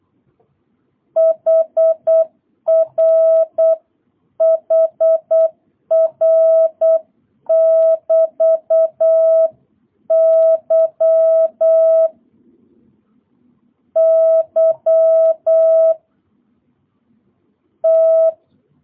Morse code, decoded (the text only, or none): HR HR <BT> Y Y T